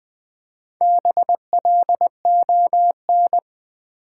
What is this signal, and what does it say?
Morse code, heard as a keyed tone, about 20 wpm, 700 Hz. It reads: BLON
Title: Build A Word - 2 Letters At A Time - 6-Letter Words - 20wpm